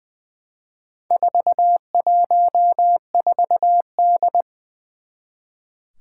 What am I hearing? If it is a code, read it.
Morse code, heard as a keyed tone, about 20 wpm, 700 Hz. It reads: 414D
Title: Four Challenging Letter-Number Combinations 20wpm